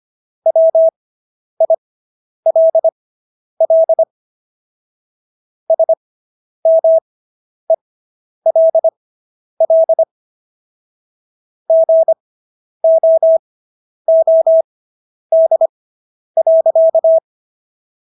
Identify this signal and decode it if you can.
Morse code, heard as a keyed tone, about 25 wpm, spaced as for 10 wpm, 650 Hz. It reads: WILL SMELL GOOD.